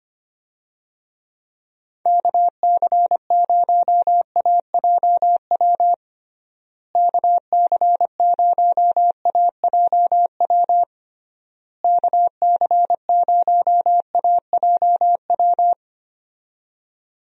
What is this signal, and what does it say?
Morse code, heard as a keyed tone, about 25 wpm, 700 Hz. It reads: KC0AJW KC0AJW KC0AJW
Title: USA Call Signs - 6 Characters - Repeated 3x - 25wpm